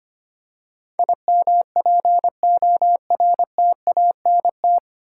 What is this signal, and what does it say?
Morse code, heard as a keyed tone, about 25 wpm, 700 Hz. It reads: IMPORTANT
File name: Nine Letter Words - 25wpm